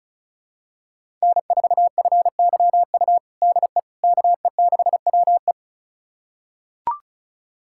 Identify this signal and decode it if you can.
Morse code, heard as a keyed tone, about 36 wpm, 700 Hz. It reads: N4FYU DE KE6WE
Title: USA Callsign DE Callsign - 5 Characters - 35wpm